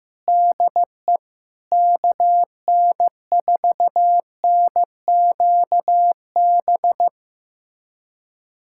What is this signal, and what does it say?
Morse code, heard as a keyed tone, about 15 wpm, 700 Hz. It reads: DE KN4NQB
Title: USA Callsign DE Callsign 15wpm